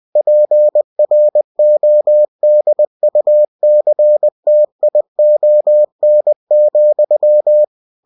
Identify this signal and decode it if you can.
Morse code, heard as a keyed tone, about 20 wpm, 600 Hz. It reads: PRODUCTION,
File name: cw-books / winnie-the-pooh / winnie-20wpm-page-0120